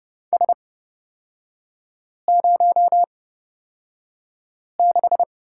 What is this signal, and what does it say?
Morse code, heard as a keyed tone, about 30 wpm, spaced as for 5 wpm, 700 Hz. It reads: S06
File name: Lesson 37e - Letter x - Three Characters 30wpm